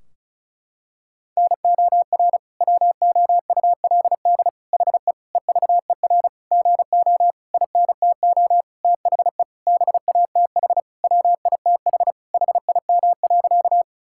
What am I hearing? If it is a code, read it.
Morse code, heard as a keyed tone, about 36 wpm, 700 Hz. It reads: NOR WOULD HE EVER GO INTO THE BATH WITH HIM.
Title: Sentences from Top 1600 Words 35wpm